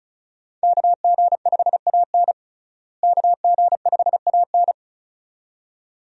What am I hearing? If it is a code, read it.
Morse code, heard as a keyed tone, about 36 wpm, 700 Hz. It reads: KG5AN KG5AN
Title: USA Call Signs - 5 Characters - Repeated 2x - 35wpm